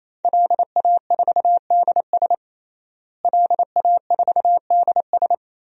Morse code, heard as a keyed tone, about 28 wpm, 700 Hz. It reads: LA4DS LA4DS
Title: DXCC-100 Call Signs - 5 Characters - Repeated 3x - 28wpm